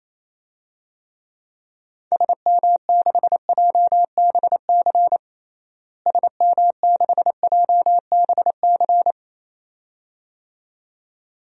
Morse code, heard as a keyed tone, about 28 wpm, 700 Hz. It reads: SM6JBC SM6JBC
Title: DXCC-100 Call Signs - 6 Characters - Repeated 2x - 28wpm